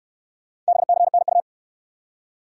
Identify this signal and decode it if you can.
Morse code, heard as a keyed tone, about 62 wpm, 700 Hz. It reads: BCNU